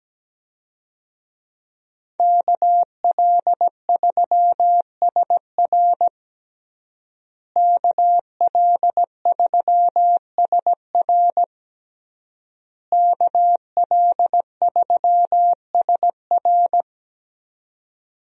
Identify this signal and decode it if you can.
Morse code, heard as a keyed tone, about 17 wpm, 700 Hz. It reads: KL3SR KL3SR KL3SR
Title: USA Call Signs - 5 Characters - Repeated 3x - 17wpm